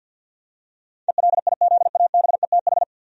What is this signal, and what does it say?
Morse code, heard as a keyed tone, about 50 wpm, 700 Hz. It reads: ELIZABETH
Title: USA Common Names - 50wpm